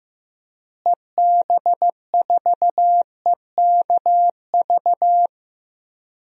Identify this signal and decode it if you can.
Morse code, heard as a keyed tone, about 15 wpm, 700 Hz. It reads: EB4EKV
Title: DXCC-100 Call Signs - 6 Characters - Repeated 3x - 15wpm